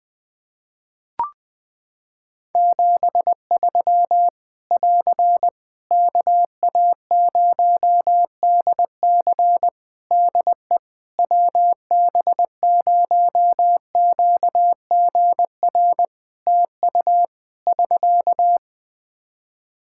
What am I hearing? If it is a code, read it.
Morse code, heard as a keyed tone, about 20 wpm, 700 Hz. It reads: 73 <AR> KA0DC DE WB0QGR TU <SK>
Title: Practice QSO - Exchange 4 - Ending - 20wpm